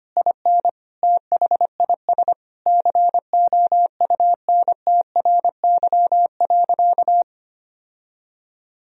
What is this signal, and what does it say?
Morse code, heard as a keyed tone, about 25 wpm, 700 Hz. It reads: IN THIS COUNTRY.
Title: Sentences from Top 1000 Words - Review Part 3 - 25wpm